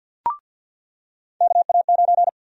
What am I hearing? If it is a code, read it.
Morse code, heard as a keyed tone, about 48 wpm, 700 Hz. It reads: KA9
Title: USA Call Sign Prefixes - 50wpm